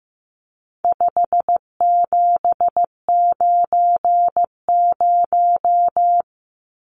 Morse code, heard as a keyed tone, about 15 wpm, 700 Hz. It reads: 5790
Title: Numbers - 4-Digit 15wpm